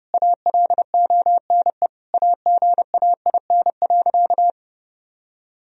Morse code, heard as a keyed tone, about 30 wpm, 700 Hz. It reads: ALONE AGAIN.
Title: Sentences from Top 1000 Words - Review Part 1 - 30wpm